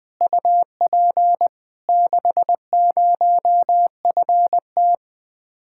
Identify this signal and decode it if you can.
Morse code, heard as a keyed tone, about 21 wpm, 700 Hz. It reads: UP 60FT
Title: Speed-Racing Common QSO Phrases Encore - 20wpm